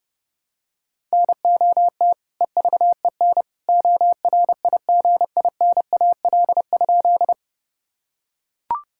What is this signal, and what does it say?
Morse code, heard as a keyed tone, about 30 wpm, 700 Hz. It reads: NOT EVEN ORIGINAL?
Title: Sentences from Top 1000 Words - Review Part 6 - 30wpm